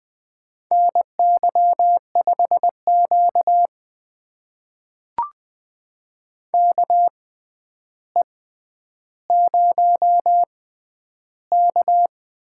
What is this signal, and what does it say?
Morse code, heard as a keyed tone, about 20 wpm, spaced as for 10 wpm, 700 Hz. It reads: NY5Q KE0K
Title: USA Call Signs - 6x Character Spacing - 20wpm